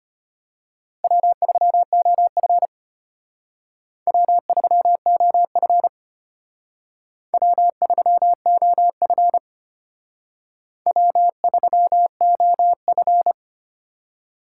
Morse code, heard as a keyed tone, about 32 wpm, 700 Hz. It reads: W3OF W3OF W3OF W3OF